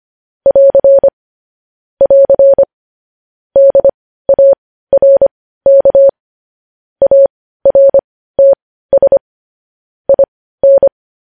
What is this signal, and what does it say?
Morse code, heard as a keyed tone, about 25 wpm, spaced as for 15 wpm, 550 Hz. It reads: <AR> <AR> DARK ARTS IN
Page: News headlines in Morse code at 15 wpm